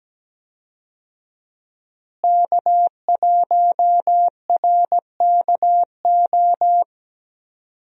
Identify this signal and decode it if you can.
Morse code, heard as a keyed tone, about 17 wpm, 700 Hz. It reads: K1RKO
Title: USA Call Signs - 5 Characters - Repeated 2x - 17wpm